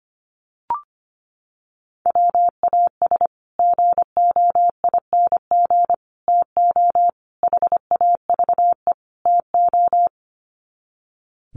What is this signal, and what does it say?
Morse code, heard as a keyed tone, about 25 wpm, 700 Hz. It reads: WAS GOING TO HAVE TO
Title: Sets of 5 Words Encore 25wpm